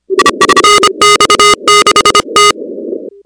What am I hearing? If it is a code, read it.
Morse code, heard as a keyed tone, about 26 wpm, 1240 Hz, with noise. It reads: NX6T
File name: contests / 2019-cw-wpx-kt5j-k5pi- -2330 0-16